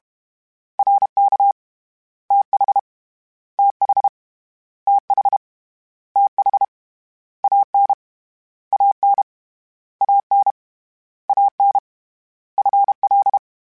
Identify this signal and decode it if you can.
Morse code, heard as a keyed tone, about 32 wpm, 800 Hz. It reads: RK TH TH TH TH AN AN AN AN FL
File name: Contrasts / Contrasts-mp3s / 2LtrContrasts @30